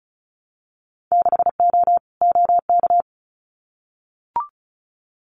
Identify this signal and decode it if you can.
Morse code, heard as a keyed tone, about 36 wpm, 700 Hz. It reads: 6O OK
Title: Winter Field Day Exchanges Encore 35wpm